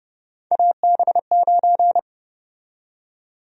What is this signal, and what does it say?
Morse code, heard as a keyed tone, about 32 wpm, 700 Hz. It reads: AB9
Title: USA Call Sign Prefixes - 30wpm